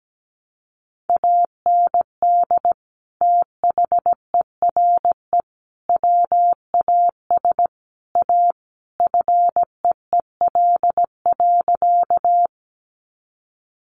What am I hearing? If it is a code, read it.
Morse code, heard as a keyed tone, about 17 wpm, 700 Hz. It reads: AND THERE WAS A FEEL.
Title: A Sound Of Thunder 17wpm